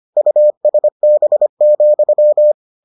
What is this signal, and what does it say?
Morse code, heard as a keyed tone, about 25 wpm, 600 Hz. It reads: USB,